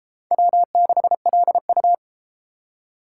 Morse code, heard as a keyed tone, about 32 wpm, 700 Hz. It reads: W6LU